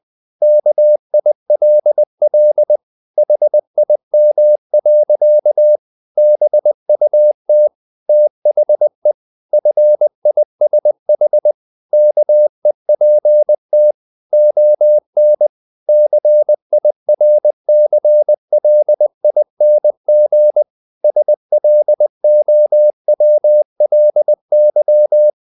Morse code, heard as a keyed tone, about 20 wpm, 600 Hz. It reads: KILL HIM. BUT THE FISH KEPT ON CIRCLING SLOWLY
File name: the old man and the sea part 2 20wpm